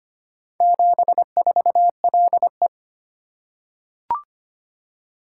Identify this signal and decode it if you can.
Morse code, heard as a keyed tone, about 25 wpm, 690 Hz. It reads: 74LE
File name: practice / 25wpm / Four Letter-Number Combinations - 25wpm